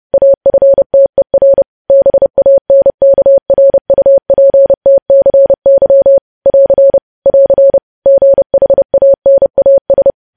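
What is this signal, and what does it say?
Morse code, heard as a keyed tone, about 30 wpm, 550 Hz. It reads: AFTER BANKRUPTCY <AR> <AR> GHANAS
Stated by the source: Morse news headlines at 30 wpm.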